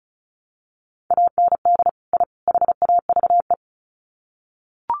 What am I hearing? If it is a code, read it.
Morse code, heard as a keyed tone, about 34 wpm, 700 Hz. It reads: AND I HAVE
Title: Sets of 3 Words 35wpm